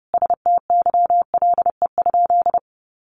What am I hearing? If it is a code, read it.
Morse code, heard as a keyed tone, about 30 wpm, 700 Hz. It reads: STYLE?
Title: Sentences from Top 800 Words 30wpm